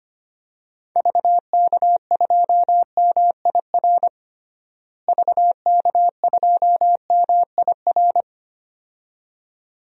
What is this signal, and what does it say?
Morse code, heard as a keyed tone, about 25 wpm, 700 Hz. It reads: VK2MIR VK2MIR
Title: DXCC-100 Call Signs - 6 Characters - Repeated 2x - 25wpm